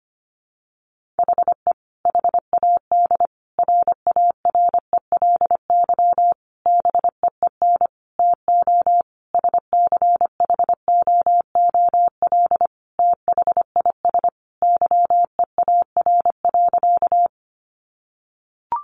Morse code, heard as a keyed tone, about 25 wpm, 700 Hz. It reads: HE HAD RARELY BEEN TO SCHOOL THIS YEAR.
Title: Sentences from Top 1400 Words 25wpm